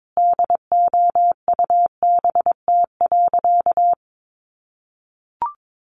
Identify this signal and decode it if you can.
Morse code, heard as a keyed tone, about 22 wpm, 700 Hz. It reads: DOUBT.
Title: Sentences from Top 1500 Words 22wpm